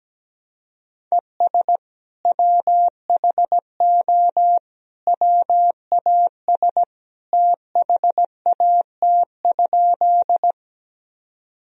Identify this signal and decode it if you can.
Morse code, heard as a keyed tone, about 17 wpm, 700 Hz. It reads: ES WHO WAS THAT?
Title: Sentences from Top 100 Words and Abbreviations 17wpm